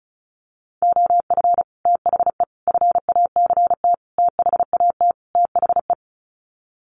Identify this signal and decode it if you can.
Morse code, heard as a keyed tone, about 34 wpm, 700 Hz. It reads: OF THE FACT THAT THE
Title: Sets of 5 Words Encore2 35wpm